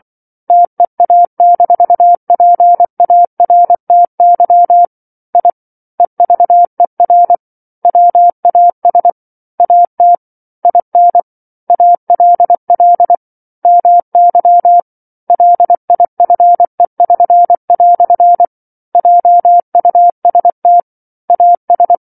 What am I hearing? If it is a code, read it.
Morse code, heard as a keyed tone, about 24 wpm, 700 Hz. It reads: TEA-PARTY I EVER WAS AT IN ALL MY LIFE<SN>" JUST AS